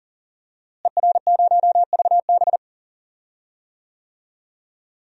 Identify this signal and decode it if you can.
Morse code, heard as a keyed tone, about 40 wpm, 700 Hz. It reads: ER0VB
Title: DXCC-100 Call Signs - 5 Characters - Repeated 3x - 40wpm